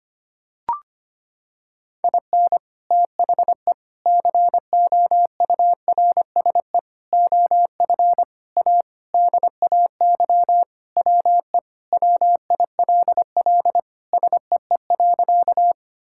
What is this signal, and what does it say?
Morse code, heard as a keyed tone, about 25 wpm, 700 Hz. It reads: IN THE COURSE OF A DAY WE WILL SEE.
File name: practice / 25wpm / Sentences with 3-Word Phrases - 25wpm